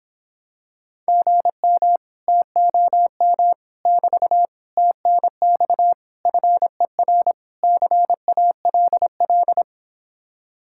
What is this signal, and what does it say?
Morse code, heard as a keyed tone, about 26 wpm, 700 Hz. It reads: GM TOM <BT> TNX FER CALL